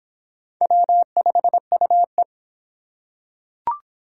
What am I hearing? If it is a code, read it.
Morse code, heard as a keyed tone, about 26 wpm, 700 Hz. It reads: W5UE